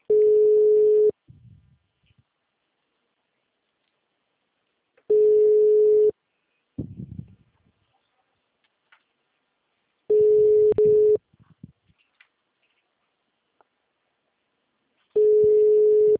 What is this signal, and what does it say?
Ringback tone (Europe, 425 Hz).
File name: out- -2008- - - .26415